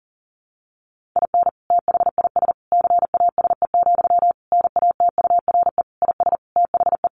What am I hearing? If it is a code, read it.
Morse code, heard as a keyed tone, about 40 wpm, 700 Hz. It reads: IN THIS CASE, NATURE IS THE
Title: Sentences from Top 1400 Words 40wpm